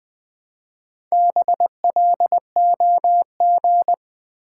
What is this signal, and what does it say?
Morse code, heard as a keyed tone, about 20 wpm, 700 Hz. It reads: BLOG
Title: Build A Word - 2 Letters At A Time - 4-Letter Words - 20wpm